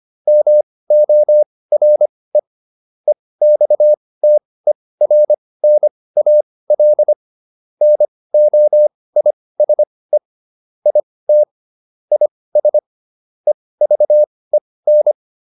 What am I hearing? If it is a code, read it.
Morse code, heard as a keyed tone, about 25 wpm, spaced as for 18 wpm, 600 Hz. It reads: MORE EXTERNAL NOISE IT IS EVEN